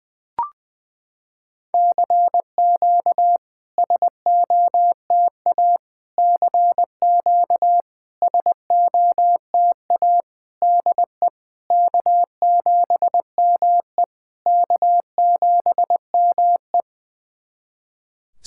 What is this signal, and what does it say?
Morse code, heard as a keyed tone, about 20 wpm, 700 Hz. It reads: CQ SOTA CQ SOTA DE K7ME K7ME
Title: SOTA QSO Practice 20wpm